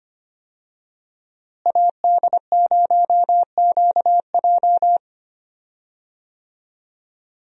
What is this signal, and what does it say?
Morse code, heard as a keyed tone, about 25 wpm, 700 Hz. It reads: AD0QJ
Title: USA Call Signs - 5 Characters - Repeated 2x - 25wpm